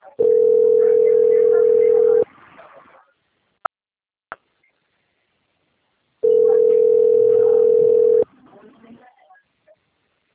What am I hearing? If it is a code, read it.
Ringback tone (North America), 440 + 480 Hz, with speech.
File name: out- - - - - .20886